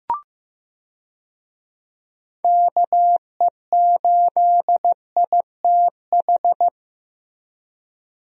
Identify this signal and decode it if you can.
Morse code, heard as a keyed tone, about 15 wpm, 700 Hz. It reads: KE8ITH
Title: USA Call Signs - 6 Characters - Repeated 2x - 15wpm